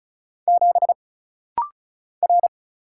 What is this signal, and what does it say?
Morse code, heard as a keyed tone, about 36 wpm, 700 Hz. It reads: 7 R
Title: Single Letter-Number Alt - Rapid-Fire 35wpm